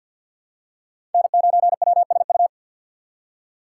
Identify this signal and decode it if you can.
Morse code, heard as a keyed tone, about 49 wpm, 700 Hz. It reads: N9WSU
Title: USA Call Signs - 5 Characters - 4x Character Spacing - 50wpm